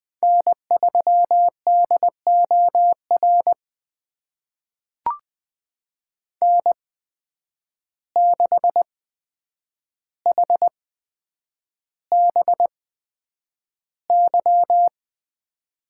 Morse code, heard as a keyed tone, about 20 wpm, spaced as for 9 wpm, 700 Hz. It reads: N3DOR N6HBY
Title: USA Call Signs - 5 Characters - 8x Character Spacing - 20wpm